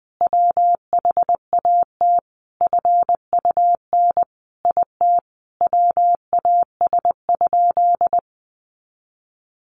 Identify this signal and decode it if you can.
Morse code, heard as a keyed tone, about 20 wpm, 700 Hz. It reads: WHAT FUN IT WAS?
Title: Sentences from Top 700 Words 20wpm